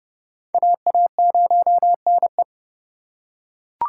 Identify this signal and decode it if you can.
Morse code, heard as a keyed tone, about 29 wpm, 700 Hz. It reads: AA0NE